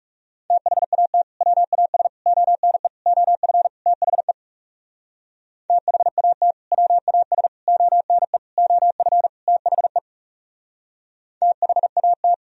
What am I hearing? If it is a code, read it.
Morse code, heard as a keyed tone, about 42 wpm, 700 Hz. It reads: THAT WAS ONE OF THE THAT WAS ONE OF THE THAT